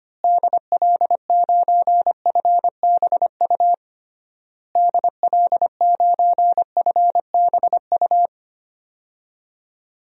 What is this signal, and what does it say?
Morse code, heard as a keyed tone, about 25 wpm, 690 Hz. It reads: DL9FBU DL9FBU
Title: DXCC-100 Call Signs - 6 Characters - Repeated 3x - 25wpm